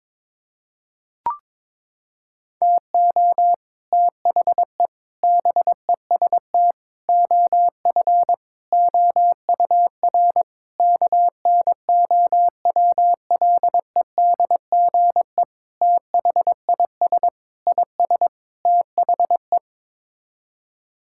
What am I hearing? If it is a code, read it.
Morse code, heard as a keyed tone, about 22 wpm, 700 Hz. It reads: TO THE BEST OF OUR KNOWLEDGE THIS IS THE
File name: Sets of 9 Words - 22wpm